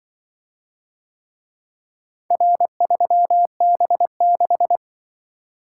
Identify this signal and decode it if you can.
Morse code, heard as a keyed tone, about 24 wpm, 700 Hz. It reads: R3B6
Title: Speed-Racing - Four Letter-Number Combinations - 20wpm